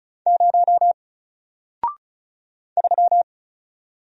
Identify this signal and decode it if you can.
Morse code, heard as a keyed tone, about 34 wpm, 700 Hz. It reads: 0 3
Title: Single Letter-Number Alt 35wpm